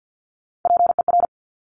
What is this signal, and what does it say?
Morse code, heard as a keyed tone, about 50 wpm, 700 Hz. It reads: PER
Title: Top Three Letter Combinations in Words 50wpm